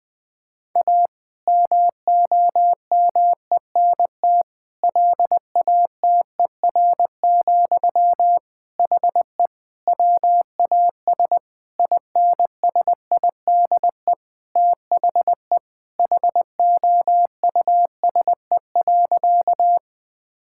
Morse code, heard as a keyed tone, about 20 wpm, 700 Hz. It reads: A MOMENT LATER, HE WAS INSIDE THE HOUSE.
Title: Sentences from Top 1000 Words - Review Part 6 - 20wpm